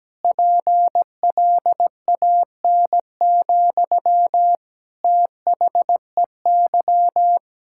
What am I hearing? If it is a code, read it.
Morse code, heard as a keyed tone, about 17 wpm, 700 Hz. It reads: PLAN, THEY
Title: Sentences from Top 1000 Words - Review Part 5 - 17wpm